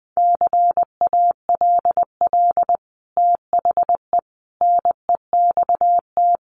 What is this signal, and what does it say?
Morse code, heard as a keyed tone, about 20 wpm, 700 Hz. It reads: CALL THE NEXT
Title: Sentences from Top 300 Words 20wpm